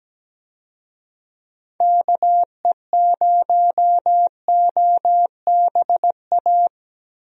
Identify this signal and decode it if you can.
Morse code, heard as a keyed tone, about 17 wpm, 700 Hz. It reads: KE0OBA
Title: USA Call Signs - 6 Characters - Repeated 2x - 17wpm